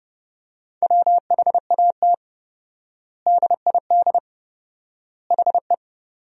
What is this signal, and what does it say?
Morse code, heard as a keyed tone, about 30 wpm, 700 Hz. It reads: WHAT DID HE
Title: 4x Word-Spacing - Sentences from Top 200 Words - 30wpm